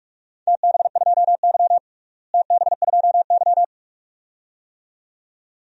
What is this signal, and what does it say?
Morse code, heard as a keyed tone, about 46 wpm, 700 Hz. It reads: TB2Y TB2Y
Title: DXCC-100 Call Signs - 4 Characters - Repeated 2x - 45wpm